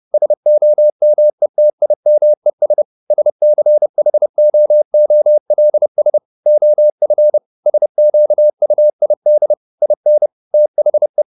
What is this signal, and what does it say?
Morse code, heard as a keyed tone, about 30 wpm, 600 Hz. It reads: SOMETIMES SCHOOLS OF SQUID IN THE
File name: the old man and the sea / the old man and the sea 1 30wpm